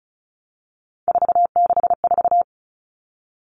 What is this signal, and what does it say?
Morse code, heard as a keyed tone, about 34 wpm, 700 Hz. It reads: 464
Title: Numbers - 3-Digit 35wpm